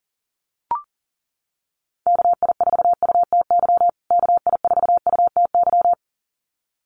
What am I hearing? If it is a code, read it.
Morse code, heard as a keyed tone, about 40 wpm, 700 Hz. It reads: KI4UTY KI4UTY
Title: USA Call Signs Repeated 2x 40wpm